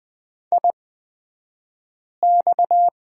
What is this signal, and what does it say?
Morse code, heard as a keyed tone, about 20 wpm, spaced as for 5 wpm, 700 Hz. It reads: IX
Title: 254 Lesson 41d - Period - Two Characters 20wpm